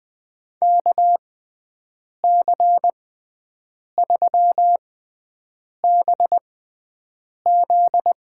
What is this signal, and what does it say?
Morse code, heard as a keyed tone, about 20 wpm, spaced as for 7 wpm, 700 Hz. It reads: KC3BZ
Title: USA Call Signs - 6 Characters - 6x Character Spacing - 20wpm